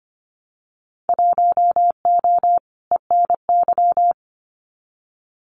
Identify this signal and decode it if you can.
Morse code, heard as a keyed tone, about 25 wpm, 700 Hz. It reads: 1O ENY
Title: Winter Field Day Exchanges 25wpm